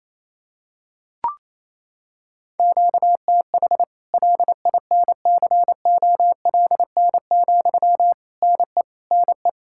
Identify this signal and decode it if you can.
Morse code, heard as a keyed tone, about 28 wpm, 700 Hz. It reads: QTH LINCOLN, NE NE
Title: Practice QSO - Exchange 1 - Beginning - 28wpm